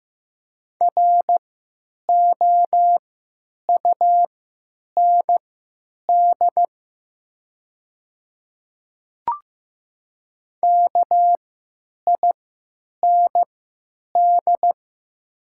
Morse code, heard as a keyed tone, about 15 wpm, spaced as for 8 wpm, 700 Hz. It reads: ROUND KIND
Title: 3x Character Spacing - Top 200 Words - 15wpm